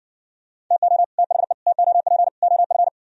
Morse code, heard as a keyed tone, about 60 wpm, 700 Hz. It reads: TO THE TOP OF